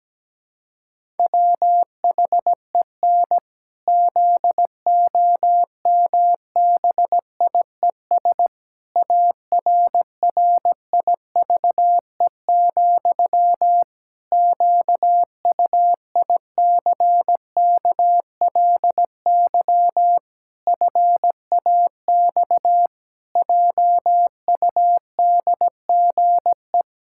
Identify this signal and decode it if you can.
Morse code, heard as a keyed tone, about 17 wpm, 700 Hz. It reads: WHEN ZOMBIES ARRIVE, QUICKLY FAX JUDGE
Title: Pangram Challenge v2 - 17wpm